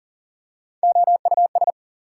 Morse code, heard as a keyed tone, about 40 wpm, 700 Hz. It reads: OUS
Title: Prefixes and Suffixes 40wpm